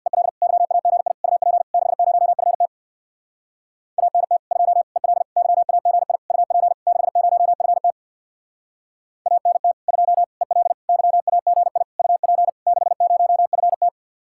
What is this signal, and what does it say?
Morse code, heard as a keyed tone, about 65 wpm, 700 Hz. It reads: EL YAGI UP 60FT ANT 2 EL YAGI UP 60FT ANT 2 EL YAGI UP 60FT